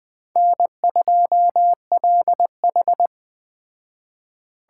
Morse code, heard as a keyed tone, about 19 wpm, 700 Hz. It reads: N2LH